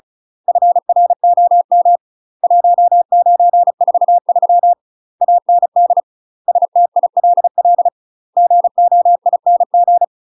Morse code, heard as a keyed tone, about 34 wpm, 700 Hz. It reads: FROM 1943 AND STILL GOING
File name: UK MASTER QSOs TRAINING / 35WPM / UK MASTER QSOs 0019